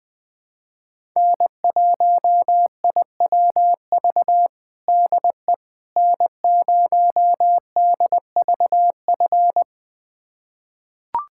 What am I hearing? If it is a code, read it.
Morse code, heard as a keyed tone, about 20 wpm, 700 Hz. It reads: N1IWV DE N0DVF
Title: USA Callsign DE Callsign - 5 Characters - 20wpm